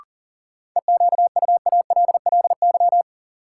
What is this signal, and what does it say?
Morse code, heard as a keyed tone, about 40 wpm, 700 Hz. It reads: EQUALLY
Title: Sentences from Top 1400 Words - New Word Drill - 40wpm